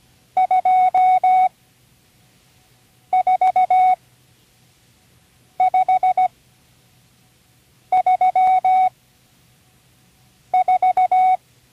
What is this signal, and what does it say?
Morse code, heard as a keyed tone, about 17 wpm, spaced as for 5 wpm, 740 Hz. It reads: 24534
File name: CW Learning Programs / Learning Morse Code (from Tune in the World) / 37 all numbers